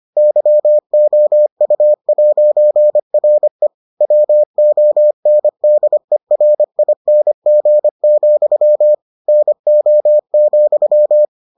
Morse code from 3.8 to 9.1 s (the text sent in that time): WONDERING,